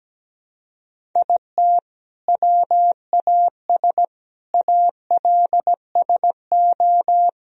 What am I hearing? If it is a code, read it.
Morse code, heard as a keyed tone, about 17 wpm, 700 Hz. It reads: IT WAS ALSO
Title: Sentences from Top 1000 Words - Review Part 1 - 17wpm